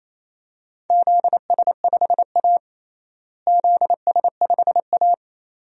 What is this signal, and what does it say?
Morse code, heard as a keyed tone, about 28 wpm, 700 Hz. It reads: ZS5A ZS5A
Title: DXCC-100 Call Signs - 4 Characters - Repeated 2x - 28wpm